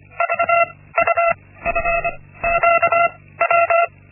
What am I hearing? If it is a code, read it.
Morse code, heard as a keyed tone, about 25 wpm, spaced as for 16 wpm, 680 Hz. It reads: VUFQW
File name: Linksmakalnis-1